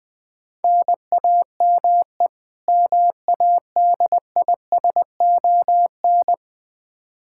Morse code, heard as a keyed tone, about 20 wpm, 700 Hz. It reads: NAME MADISON
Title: Practice QSO - Exchange 1 - Beginning - 20wpm